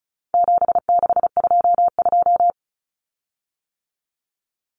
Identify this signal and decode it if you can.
Morse code, heard as a keyed tone, about 34 wpm, 700 Hz. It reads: 7622
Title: Common Cut Numbers - 4-Digit 35wpm